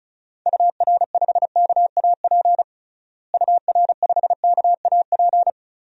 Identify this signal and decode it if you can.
Morse code, heard as a keyed tone, about 34 wpm, 700 Hz. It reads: UR5KAP UR5KAP
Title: DXCC-100 Call Signs - 6 Characters - Repeated 3x - 35wpm